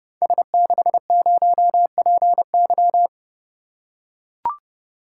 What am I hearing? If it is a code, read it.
Morse code, heard as a keyed tone, about 30 wpm, 700 Hz. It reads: S60PY
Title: Code Groups - 5 Letter:Numbers - 30wpm